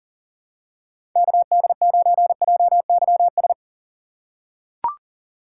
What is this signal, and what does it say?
Morse code, heard as a keyed tone, about 40 wpm, 700 Hz. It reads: KD9JYS
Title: USA Call Signs - 6 Characters - 3x Character Spacing - 40wpm